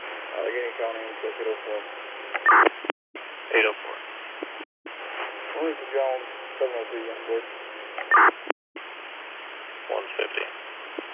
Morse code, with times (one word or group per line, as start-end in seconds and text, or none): none